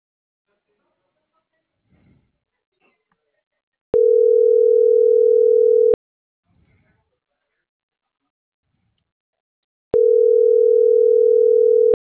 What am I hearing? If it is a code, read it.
Ringback tone (North America), 440 + 480 Hz.